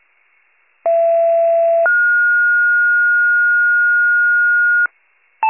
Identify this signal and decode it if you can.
Two-tone page, heard as a one-second tone, then a three-second tone, about 670 then 1430 Hz. It is heard from 0.9 to 4.9 s.